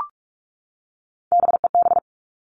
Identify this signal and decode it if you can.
Morse code, heard as a keyed tone, about 44 wpm, 700 Hz. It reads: BEB
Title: All Cut Numbers - 3-Digit 45wpm